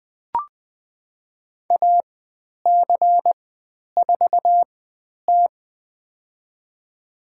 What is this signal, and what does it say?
Morse code, heard as a keyed tone, about 20 wpm, spaced as for 10 wpm, 700 Hz. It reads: AC4T
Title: 109 Lesson 18g - Letter m - Call Signs 20wpm